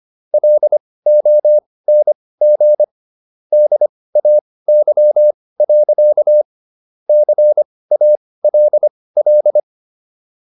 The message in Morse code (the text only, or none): LONG DAY. CALL